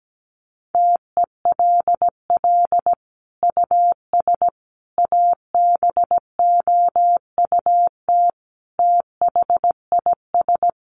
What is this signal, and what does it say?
Morse code, heard as a keyed tone, about 17 wpm, 700 Hz. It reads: TELL US ABOUT THIS
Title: Sentences from Top 1200 Words 17wpm